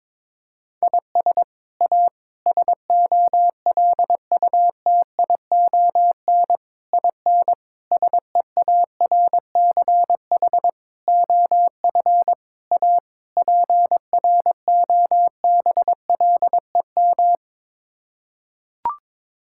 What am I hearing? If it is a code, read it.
Morse code, heard as a keyed tone, about 22 wpm, 700 Hz. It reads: IS A SOLUTION IN SEARCH OF A PROBLEM
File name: practice / 22wpm / Sets of 8 Words - 22wpm